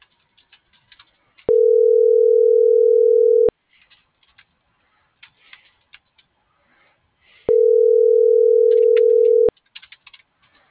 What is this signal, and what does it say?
Ringback tone (North America), 440 + 480 Hz.